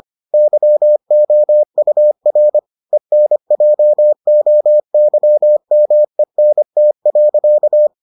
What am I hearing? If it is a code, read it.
Morse code, heard as a keyed tone, about 25 wpm, 600 Hz. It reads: YOUR ENJOYMENT.